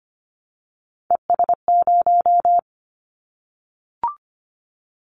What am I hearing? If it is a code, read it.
Morse code, heard as a keyed tone, about 25 wpm, 700 Hz. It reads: ES0
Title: Callsign Prefixes for Top 50 DXCC - 25wpm